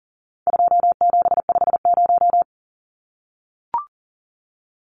Morse code, heard as a keyed tone, about 37 wpm, 700 Hz. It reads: 2750
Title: Numbers - 4-Digit 40wpm